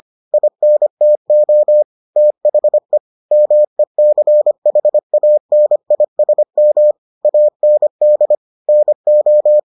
Morse code, heard as a keyed tone, about 25 wpm, 600 Hz. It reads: INTO THE MECHANISM AND NO